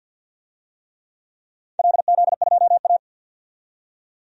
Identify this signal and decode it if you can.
Morse code, heard as a keyed tone, about 49 wpm, 700 Hz. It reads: LZ2A
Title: DXCC-100 Call Signs - 4 Characters - 50wpm